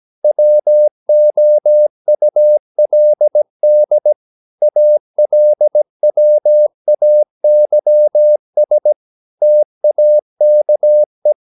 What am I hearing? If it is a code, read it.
Morse code, heard as a keyed tone, about 17 wpm, 600 Hz. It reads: WOULD ALWAYS TAKE
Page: Romanen Never af Ken Follet på engelsk.